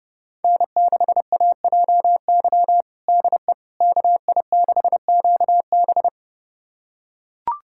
Morse code, heard as a keyed tone, about 30 wpm, 700 Hz. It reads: N6AJY DE KI6QB
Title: USA Callsign DE Callsign - 5 Characters - 30wpm